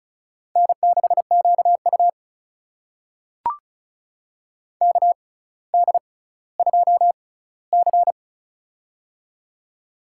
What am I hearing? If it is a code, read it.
Morse code, heard as a keyed tone, about 36 wpm, spaced as for 18 wpm, 700 Hz. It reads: N6QU KD2C
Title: USA Call Signs - 4 Characters - 6x Character Spacing - 35wpm